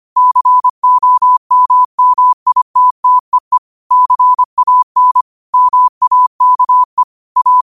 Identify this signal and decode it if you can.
Morse code, heard as a keyed tone, about 25 wpm, 1000 Hz. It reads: COMMITTEE CAN MAKE A